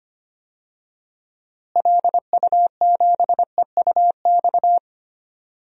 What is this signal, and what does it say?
Morse code, heard as a keyed tone, about 25 wpm, 700 Hz. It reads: LU7EUX
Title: DXCC-100 Call Signs - 6 Characters - Repeated 3x - 25wpm